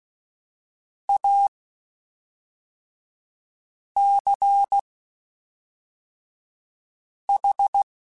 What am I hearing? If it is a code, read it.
Morse code, heard as a keyed tone, about 16 wpm, spaced as for 3 wpm, 800 Hz. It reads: ACH
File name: QSOs / QSOs-mp3s / WD0BGP 1@5